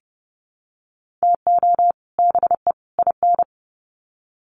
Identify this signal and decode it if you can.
Morse code, heard as a keyed tone, about 30 wpm, 700 Hz. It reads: TO BE IN
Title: Sets of 3 Words 30wpm